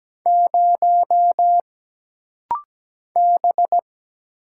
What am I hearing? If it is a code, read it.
Morse code, heard as a keyed tone, about 17 wpm, 700 Hz. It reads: 0 B
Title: Single Letter-Number Alt - Rapid-Fire 17wpm